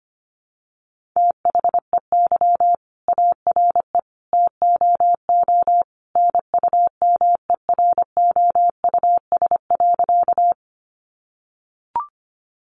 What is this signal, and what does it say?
Morse code, heard as a keyed tone, about 25 wpm, 700 Hz. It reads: THEY ARE TOO NUMEROUS.
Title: Sentences from Top 1400 Words 25wpm